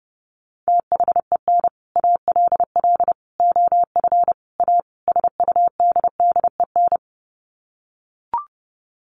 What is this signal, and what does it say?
Morse code, heard as a keyed tone, about 30 wpm, 700 Hz. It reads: THEN ALL OF A SUDDEN
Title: Sets of 5 Words Encore 30wpm